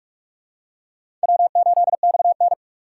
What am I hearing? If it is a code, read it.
Morse code, heard as a keyed tone, about 46 wpm, 700 Hz. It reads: W8XN
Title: USA Call Signs - 4 Characters - 6x Character Spacing - 45wpm